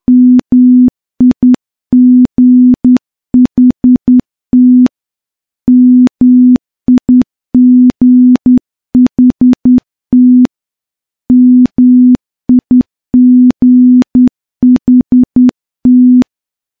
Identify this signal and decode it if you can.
Morse code, heard as a keyed tone, about 10 wpm, 250 Hz. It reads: MIGHT MIGHT MIGHT
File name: morse / morse-training-10WPM